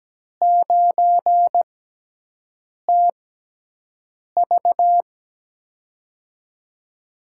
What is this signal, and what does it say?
Morse code, heard as a keyed tone, about 17 wpm, spaced as for 6 wpm, 700 Hz. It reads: 9TV
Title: USA Call Signs - 4 Characters - 6x Character Spacing - 17wpm